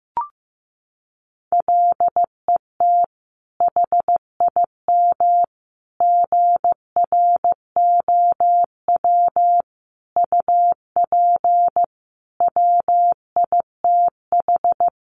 Morse code, heard as a keyed tone, about 15 wpm, 700 Hz. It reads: LET HIM GROW UP WITH
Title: Sentences from Top 300 Words 15wpm